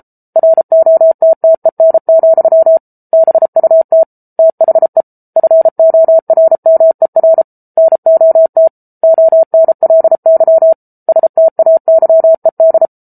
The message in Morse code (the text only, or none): ROTTEN, BUT THE FORMER NOT ONLY STAYED